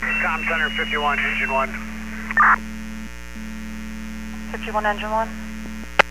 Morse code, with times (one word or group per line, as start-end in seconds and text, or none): none